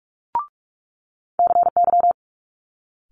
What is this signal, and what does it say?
Morse code, heard as a keyed tone, about 46 wpm, 700 Hz. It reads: CY
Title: Multi-letter Phonemes Quick Fire 45wpm